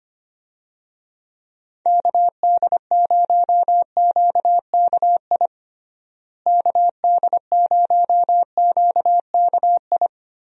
Morse code, heard as a keyed tone, about 25 wpm, 700 Hz. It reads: KD0QKI KD0QKI
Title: USA Call Signs - 6 Characters - Repeated 2x - 25wpm